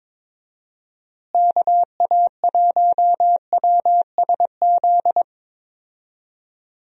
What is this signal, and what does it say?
Morse code, heard as a keyed tone, about 22 wpm, 700 Hz. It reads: KA1WSZ
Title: USA Call Signs - 6 Characters - Repeated 2x - 22wpm